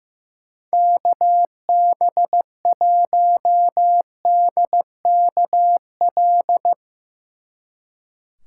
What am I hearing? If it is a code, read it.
Morse code, heard as a keyed tone, about 15 wpm, 700 Hz. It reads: KB1DKL
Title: Speed Racing - USA 2x3 Call Signs 15wpm